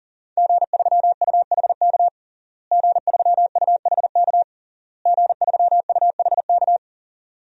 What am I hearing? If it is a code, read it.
Morse code, heard as a keyed tone, about 40 wpm, 700 Hz. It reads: G3UHK G3UHK G3UHK
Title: DXCC-100 Call Signs - 5 Characters - Repeated 3x - 40wpm